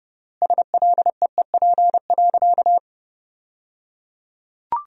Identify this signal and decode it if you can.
Morse code, heard as a keyed tone, about 29 wpm, 700 Hz. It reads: SLEEP.
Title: 3x Word-Spacing - Sentences from Top 500 Words - 30wpm